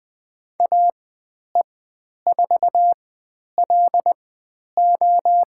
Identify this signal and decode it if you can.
Morse code, heard as a keyed tone, about 20 wpm, spaced as for 10 wpm, 700 Hz. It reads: AE4LO
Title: 109 Lesson 18g - Letter m - Call Signs 20wpm